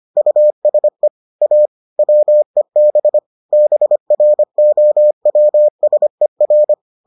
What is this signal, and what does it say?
Morse code, heard as a keyed tone, about 25 wpm, 600 Hz. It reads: USE A WEB BROWSER